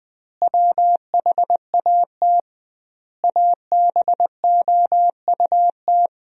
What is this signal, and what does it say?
Morse code, heard as a keyed tone, about 20 wpm, 700 Hz. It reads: WHAT ABOUT
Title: 2x Word-Spacing - Sentences from Top 500 Words - 20wpm